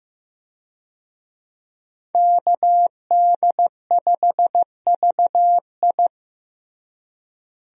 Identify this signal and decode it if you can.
Morse code, heard as a keyed tone, about 16 wpm, 700 Hz. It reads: KD5VI
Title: USA Call Signs - 5 Characters - Repeated 3x - 15wpm